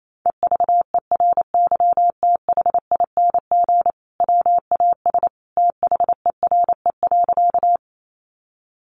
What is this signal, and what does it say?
Morse code, heard as a keyed tone, about 28 wpm, 700 Hz. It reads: EVERYTHING WAS THERE.
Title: Sentences from Top 700 Words 28wpm